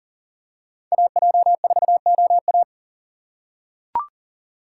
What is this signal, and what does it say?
Morse code, heard as a keyed tone, about 40 wpm, 700 Hz. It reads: AJ4OA
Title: USA Call Signs - 4x Character Spacing - 40wpm